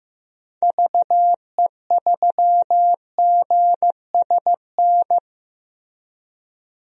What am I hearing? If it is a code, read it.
Morse code, heard as a keyed tone, about 15 wpm, 700 Hz. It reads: VE3GSN
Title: DXCC-100 Call Signs - 6 Characters - Repeated 2x - 15wpm